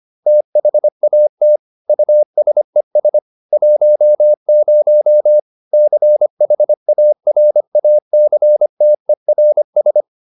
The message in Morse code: THAT USES 10 CHARACTERS

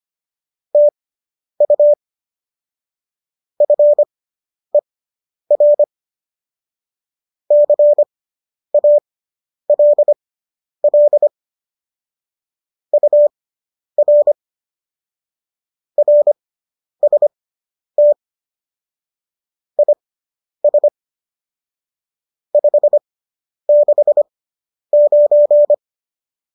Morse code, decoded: TU FER CALL UR RST IS 569

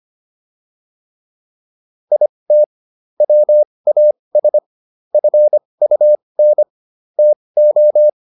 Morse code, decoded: IT WAS FUN TO

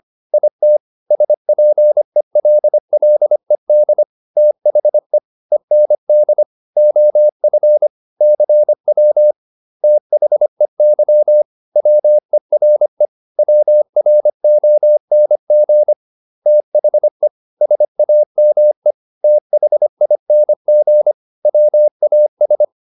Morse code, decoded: IT SPELLED THE END OF CW THEY WERE WRONG THE SAME THING WAS

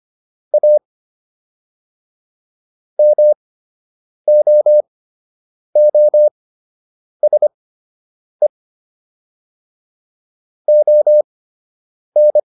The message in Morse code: A MOOSE ON

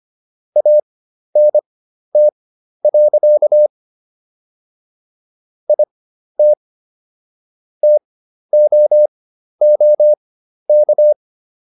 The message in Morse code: ANT. IT TOOK

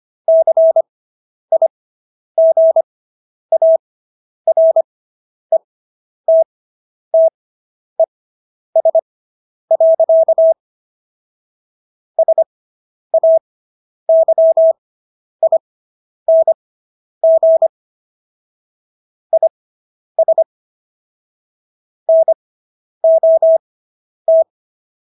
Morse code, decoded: CIGARETTES. SAYING IS NOT

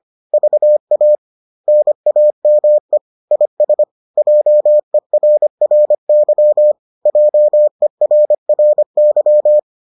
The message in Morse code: VA NAME IS JERRY JERRY

